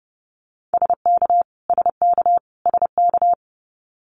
Morse code, heard as a keyed tone, about 30 wpm, 700 Hz. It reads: SK SK SK
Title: Canadian Provinces and Territories - Repeated 3x - 30wpm